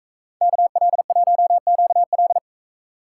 Morse code, heard as a keyed tone, about 42 wpm, 700 Hz. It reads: KL1QL